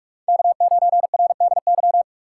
Morse code, heard as a keyed tone, about 46 wpm, 700 Hz. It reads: K9RDY
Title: USA Call Signs - 3x Character Spacing - 45wpm